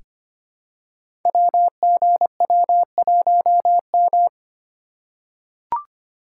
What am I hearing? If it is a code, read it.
Morse code, heard as a keyed tone, about 25 wpm, 700 Hz. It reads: WGW1M
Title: Code Groups - 5 Letter:Numbers - 25wpm